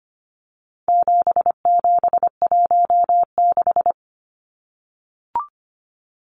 Morse code, heard as a keyed tone, about 25 wpm, 700 Hz. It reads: 7716
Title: Numbers - 4-Digit 25wpm